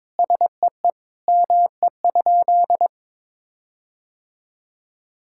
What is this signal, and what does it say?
Morse code, heard as a keyed tone, about 22 wpm, 700 Hz. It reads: SEE ME?
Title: Sentences from Top 1600 Words 22wpm